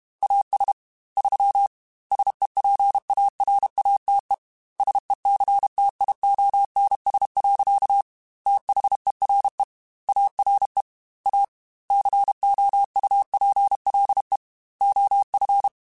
Morse code, heard as a keyed tone, about 32 wpm, 800 Hz. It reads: AS 3 SEPARATE SECTIONS. THERE ARE A COUPLE OF
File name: Technical Subjects / Technical Subjects-mp3s / Sw Pwr Sup @30